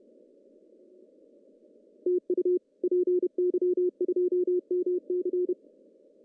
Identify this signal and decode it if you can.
Morse code, heard as a keyed tone, about 31 wpm, 370 Hz. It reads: TU PY2MC
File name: CW / cqww-cw-k5tr